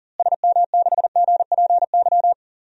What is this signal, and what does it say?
Morse code, heard as a keyed tone, about 40 wpm, 700 Hz. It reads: SM6GPY
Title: DXCC-100 Call Signs - 6 Characters - Repeated 3x - 40wpm